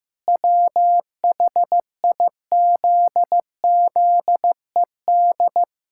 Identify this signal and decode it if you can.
Morse code, heard as a keyed tone, about 15 wpm, 700 Hz. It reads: WHIZZED
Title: Pangram Challenge v2 - 15wpm